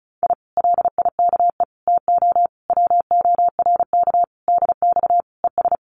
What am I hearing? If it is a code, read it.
Morse code, heard as a keyed tone, about 36 wpm, 700 Hz. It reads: I LIKE TO WORK DX ES
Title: Common QSO Phrases Encore 35wpm